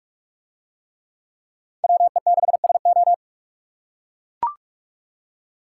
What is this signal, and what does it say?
Morse code, heard as a keyed tone, about 41 wpm, 700 Hz. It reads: WE6SO